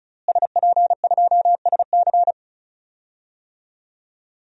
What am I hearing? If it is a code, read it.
Morse code, heard as a keyed tone, about 36 wpm, 700 Hz. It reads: SP2SC
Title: DXCC-100 Call Signs - 5 Characters - Repeated 3x - 35wpm